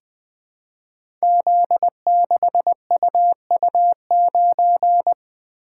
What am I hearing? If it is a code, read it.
Morse code, heard as a keyed tone, about 20 wpm, 700 Hz. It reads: Z6UU9
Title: Code Groups - 5 Letter:Numbers - 20wpm